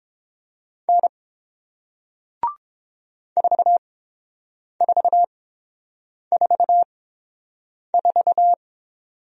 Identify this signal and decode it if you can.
Morse code, heard as a keyed tone, about 30 wpm, spaced as for 7 wpm, 690 Hz. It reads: N 4444